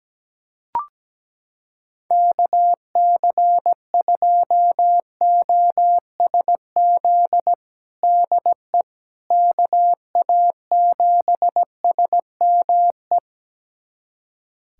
Morse code, heard as a keyed tone, about 17 wpm, 700 Hz. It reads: KC2OSZ DE KA7SME
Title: USA Callsign DE Callsign - 6 Characters - 17wpm